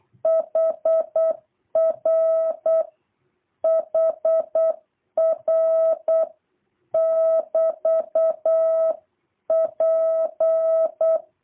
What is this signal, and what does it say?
Morse code, heard as a keyed tone, about 8 wpm, 650 Hz. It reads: HR HR <BT> P
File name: jh8cbh / 2023cw